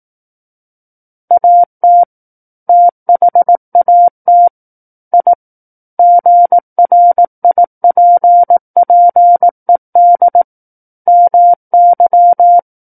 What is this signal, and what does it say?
Morse code, heard as a keyed tone, about 18 wpm, 700 Hz. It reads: AT THAT I GRIPPED MY
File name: 18 WPM